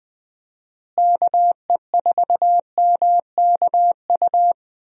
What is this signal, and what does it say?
Morse code, heard as a keyed tone, about 20 wpm, 700 Hz. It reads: KE4MKU
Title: USA Call Signs - 6 Characters - Repeated 3x - 20wpm